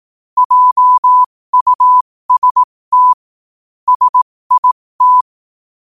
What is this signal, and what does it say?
Morse code, heard as a keyed tone, about 18 wpm, spaced as for 15 wpm, 1000 Hz. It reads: JUST SIT
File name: qotdfiles / 15wpm